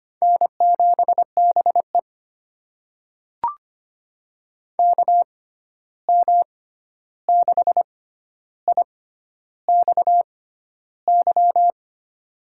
Morse code, heard as a keyed tone, about 25 wpm, spaced as for 11 wpm, 700 Hz. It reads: N7BE KM6IXY
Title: USA Call Signs - 6x Character Spacing - 25wpm